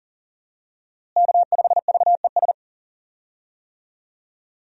Morse code, heard as a keyed tone, about 40 wpm, 700 Hz. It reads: K5VES
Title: USA Call Signs - 5 Characters - Repeated 3x - 40wpm